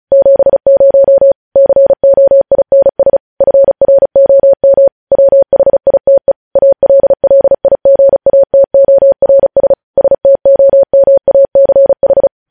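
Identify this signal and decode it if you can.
Morse code, heard as a keyed tone, about 36 wpm, 550 Hz. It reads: 70 COINS FROM WHITE ALLIGATORS STOMACH
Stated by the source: Morse news headlines sent at 35 words per minute.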